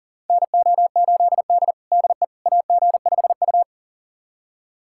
Morse code, heard as a keyed tone, about 40 wpm, 700 Hz. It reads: NO8D DE AG5U
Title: USA Callsign DE Callsign - 4 Characters - 40wpm